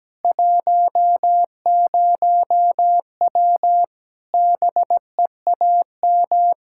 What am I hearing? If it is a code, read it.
Morse code, heard as a keyed tone, about 17 wpm, 700 Hz. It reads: 10W BEAM
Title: Practice QSO - Exchange 2 - Info - 17wpm